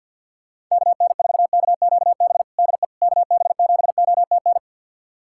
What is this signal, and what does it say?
Morse code, heard as a keyed tone, about 52 wpm, 700 Hz. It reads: KN4KQB DE KB7OTN
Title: USA Callsign DE Callsign 50wpm